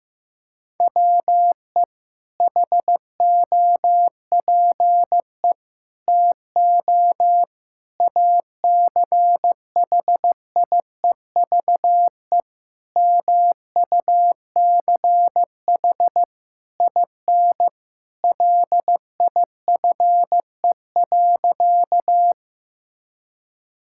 Morse code, heard as a keyed tone, about 15 wpm, 700 Hz. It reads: WE HOPE TO ACHIEVE MUCH IN LIFE.
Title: Sentences from Top 1000 Words - Review Part 6 - 15wpm